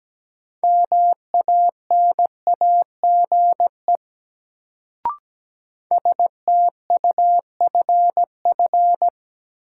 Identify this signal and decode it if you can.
Morse code, heard as a keyed tone, about 17 wpm, 700 Hz. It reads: MANAGE STUFF
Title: Sentences from Top 1400 Words - New Word Drill - 17wpm